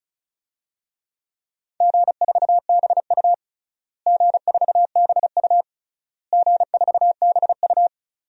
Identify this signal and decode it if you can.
Morse code, heard as a keyed tone, about 36 wpm, 700 Hz. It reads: G4BU G4BU G4BU
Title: DXCC-100 Call Signs - 4 Characters - Repeated 3x - 35wpm